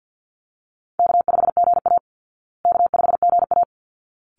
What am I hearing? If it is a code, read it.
Morse code, heard as a keyed tone, about 50 wpm, 700 Hz. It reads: K5GA K5GA
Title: Easy Call Signs Repeated 2x - 50wpm